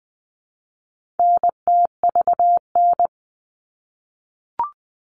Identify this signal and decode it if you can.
Morse code, heard as a keyed tone, about 20 wpm, 700 Hz. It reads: NTVN
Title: All Cut Numbers - 4-Digit 20wpm